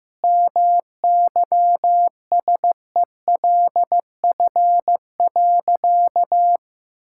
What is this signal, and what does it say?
Morse code, heard as a keyed tone, about 15 wpm, 700 Hz. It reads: MYSELF.
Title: Sentences from Top 1600 Words 15wpm